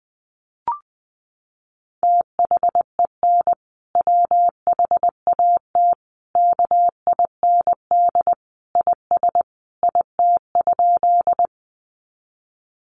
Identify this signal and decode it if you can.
Morse code, heard as a keyed tone, about 20 wpm, 700 Hz. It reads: THEN WHAT KIND IS IT?
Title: Sentences from Top 200 Words 20wpm